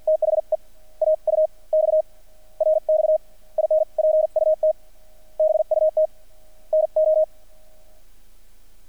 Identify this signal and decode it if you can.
Morse code, heard as a keyed tone, about 50 wpm, spaced as for 41 wpm, 620 Hz. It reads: THE AUX RX INPUT BUT NO
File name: archived / CRSnet